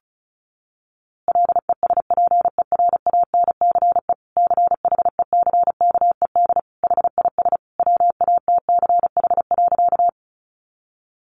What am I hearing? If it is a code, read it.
Morse code, heard as a keyed tone, about 34 wpm, 700 Hz. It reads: LESPERANCE CHECKED HIS WATCH.
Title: A Sound Of Thunder 35wpm